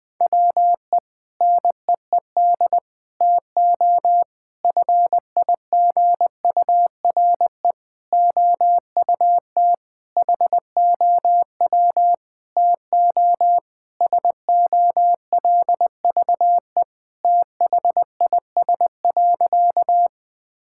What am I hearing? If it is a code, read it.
Morse code, heard as a keyed tone, about 20 wpm, 700 Hz. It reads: WE NEED TO FIGURE OUT HOW TO SOLVE THIS.
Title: Sentences with 5-Word Phrases - Spoken Beginning - 20wpm